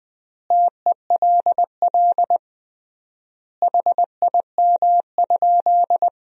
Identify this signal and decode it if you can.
Morse code, heard as a keyed tone, about 20 wpm, 700 Hz. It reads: TELL HIM?
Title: 3x Word-Spacing - Sentences from Top 200 Words - 20wpm